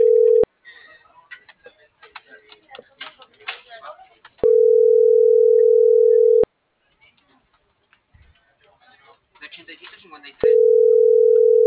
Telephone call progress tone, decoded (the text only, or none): ringback tone